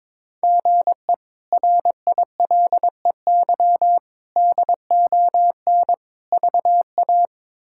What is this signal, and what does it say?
Morse code, heard as a keyed tone, about 22 wpm, 700 Hz. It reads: GE RILEY DON VA
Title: K1USN SST QSO Practice - 22wpm